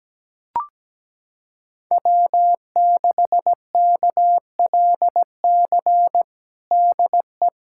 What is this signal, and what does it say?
Morse code, heard as a keyed tone, about 17 wpm, 700 Hz. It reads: W6KLC DE
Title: USA Callsign DE Callsign - 5 Characters - 17wpm